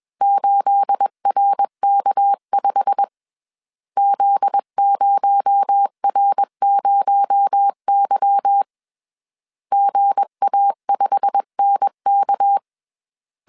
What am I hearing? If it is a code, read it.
Morse code, heard as a keyed tone, about 21 wpm, 800 Hz. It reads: 8RK5 Z0R0Y GA5NK